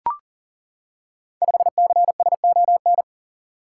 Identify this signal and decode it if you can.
Morse code, heard as a keyed tone, about 39 wpm, 700 Hz. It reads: 5CSON
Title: Code Groups - 5 Letter:Numbers - 40wpm